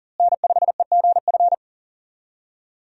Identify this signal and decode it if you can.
Morse code, heard as a keyed tone, about 40 wpm, 700 Hz. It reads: N5EGF
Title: USA Call Signs - 5 Characters - Repeated 2x - 40wpm